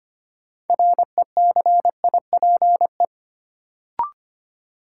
Morse code, heard as a keyed tone, about 25 wpm, 700 Hz. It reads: RECIPE
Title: Sentences from Top 1200 Words - New Word Drill Alt - 25wpm